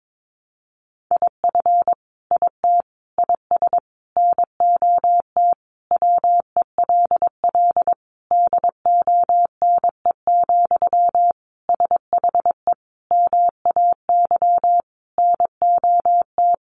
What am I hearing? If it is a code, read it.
Morse code, heard as a keyed tone, about 22 wpm, 700 Hz. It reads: IF IT IS NOT WELL DONE, SHE MAY NOT
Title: Sentences from Top 1200 Words 22wpm